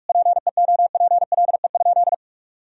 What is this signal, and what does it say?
Morse code, heard as a keyed tone, about 41 wpm, 700 Hz. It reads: PEOPLE?
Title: Sentences from Top 1000 Words - Review Part 3 - 45wpm